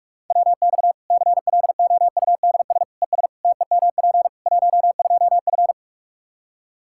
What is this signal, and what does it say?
Morse code, heard as a keyed tone, about 41 wpm, 700 Hz. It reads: WX CLOUDS ES TEMP 12F
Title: Practice QSO - Exchange 2 - Info - 45wpm